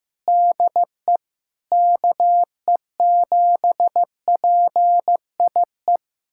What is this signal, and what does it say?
Morse code, heard as a keyed tone, about 15 wpm, 700 Hz. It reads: DE KE7PIE
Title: USA Callsign DE Callsign - 6 Characters - 15wpm